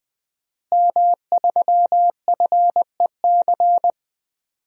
Morse code, heard as a keyed tone, about 20 wpm, 700 Hz. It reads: M3FEC
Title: DXCC-100 Call Signs - 5 Characters - Repeated 2x - 20wpm